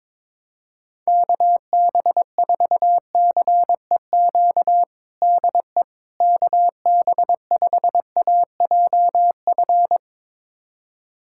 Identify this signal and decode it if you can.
Morse code, heard as a keyed tone, about 22 wpm, 700 Hz. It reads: KB4CEQ DE KB5AJF
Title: USA Callsign DE Callsign 22wpm